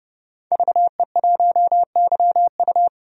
Morse code, heard as a keyed tone, about 31 wpm, 700 Hz. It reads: VE1YU
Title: DXCC-100 Call Signs - 5 Characters - 30wpm